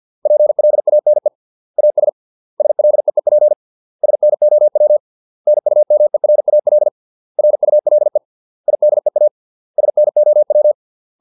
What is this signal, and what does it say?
Morse code, heard as a keyed tone, about 50 wpm, 600 Hz. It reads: PLANE AS SLEEP SNOW NUMERAL RULE IDEA SNOW